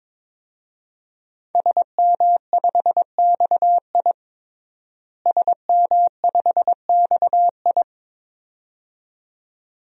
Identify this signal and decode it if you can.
Morse code, heard as a keyed tone, about 22 wpm, 700 Hz. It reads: SM5XI SM5XI
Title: DXCC-100 Call Signs - 5 Characters - Repeated 2x - 22wpm